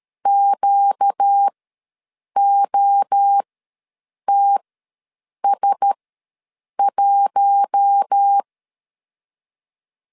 Morse code, heard as a keyed tone, about 13 wpm, spaced as for 7 wpm, 800 Hz. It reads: QOTS1
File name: LEZIONI / 7 GF9Z / 7 GF9 ws 12wpm 3344